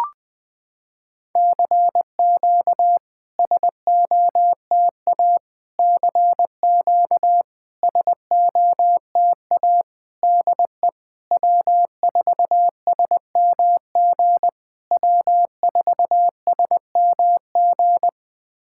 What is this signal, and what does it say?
Morse code, heard as a keyed tone, about 20 wpm, 700 Hz. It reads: CQ SOTA CQ SOTA DE W4SMG W4SMG
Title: SOTA QSO Practice 20wpm